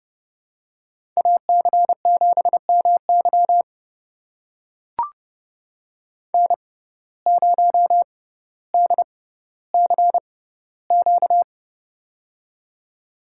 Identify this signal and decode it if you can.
Morse code, heard as a keyed tone, about 32 wpm, spaced as for 16 wpm, 700 Hz. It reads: AC7MY N0DCQ
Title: USA Call Signs - 5 Characters - 6x Character Spacing - 30wpm